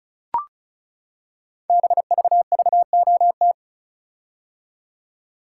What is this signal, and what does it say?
Morse code, heard as a keyed tone, about 36 wpm, 700 Hz. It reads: BVVOT
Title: Code Groups - 5 Letter:Numbers - 35wpm